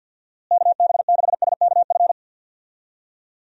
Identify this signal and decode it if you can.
Morse code, heard as a keyed tone, about 51 wpm, 700 Hz. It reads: KB6SKF
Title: USA Call Signs - 6 Characters - Repeated 2x - 50wpm